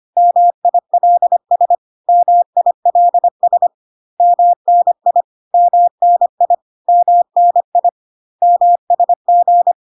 Morse code, heard as a keyed tone, about 25 wpm, 700 Hz. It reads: MILS MILS MNI MNI MNI MSG
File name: CW abbreviations alphabetical repeated 3 times 25 25wpm